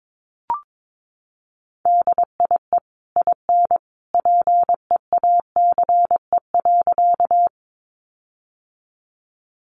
Morse code, heard as a keyed tone, about 22 wpm, 700 Hz. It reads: DIE IN PEACE.
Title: Sentences from Top 1500 Words 22wpm